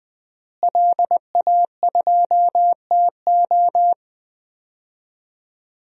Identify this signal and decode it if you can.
Morse code, heard as a keyed tone, about 20 wpm, 700 Hz. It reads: LA2TO
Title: DXCC-100 Call Signs - 5 Characters - Repeated 2x - 20wpm